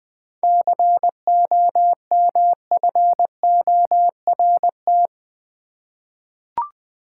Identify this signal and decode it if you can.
Morse code, heard as a keyed tone, about 20 wpm, 700 Hz. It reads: COMFORT
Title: Seven Letter Words - 20wpm